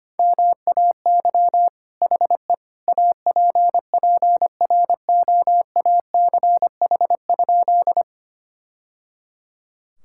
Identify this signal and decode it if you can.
Morse code, heard as a keyed tone, about 25 wpm, 700 Hz. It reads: MAY HE APPROACH?
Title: Sentences from Top 1000 Words - Review Part 6 - 25wpm